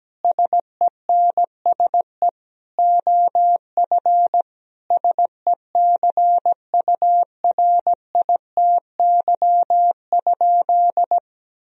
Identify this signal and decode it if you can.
Morse code, heard as a keyed tone, about 17 wpm, 700 Hz. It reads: SENSE OF SECURITY?
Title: Sentences from Top 1000 Words - Review Part 4 - 17wpm